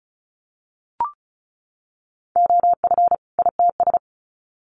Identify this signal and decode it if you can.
Morse code, heard as a keyed tone, about 34 wpm, 700 Hz. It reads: OF ITS
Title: Sets of 2 Words 35wpm